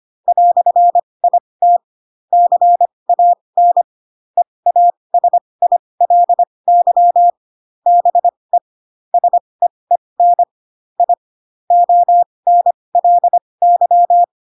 Morse code, heard as a keyed tone, about 25 wpm, spaced as for 20 wpm, 700 Hz. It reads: "IT CAN EASILY BE SEEN I ONLY
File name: Pinocchio / Pinocchio ch1 15 25 ik1hge